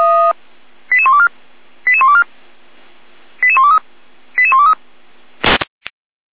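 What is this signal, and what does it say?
Five-tone selective call (ZVEI), heard as quick runs of five tones, tones of about 70 ms. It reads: 88124 88124 88123 88123
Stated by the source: ZVEI five-tone call received by SDR.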